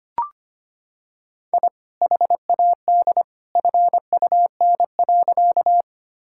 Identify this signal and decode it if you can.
Morse code, heard as a keyed tone, about 25 wpm, 700 Hz. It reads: I HAD FUN.
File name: practice / 25wpm / Sentences with 2-Word Phrases - 25wpm